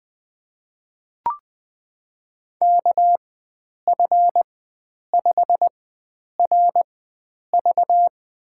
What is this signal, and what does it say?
Morse code, heard as a keyed tone, about 20 wpm, spaced as for 9 wpm, 700 Hz. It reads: KF5RV
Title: USA Call Signs - 4x Character Spacing - 20wpm